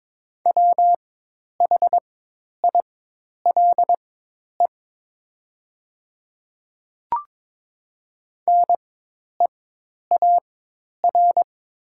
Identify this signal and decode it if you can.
Morse code, heard as a keyed tone, about 22 wpm, spaced as for 10 wpm, 700 Hz. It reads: WHILE NEAR
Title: 4x Character Spacing - Top 500 Words - 22wpm